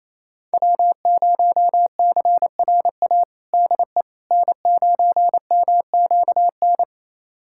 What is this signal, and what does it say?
Morse code, heard as a keyed tone, about 28 wpm, 700 Hz. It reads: W0CRA DE N9MQN
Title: USA Callsign DE Callsign - 5 Characters - 28wpm